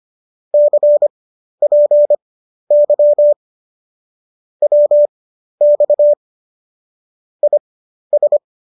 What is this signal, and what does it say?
Morse code, heard as a keyed tone, about 25 wpm, spaced as for 12 wpm, 590 Hz. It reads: CPY WX IS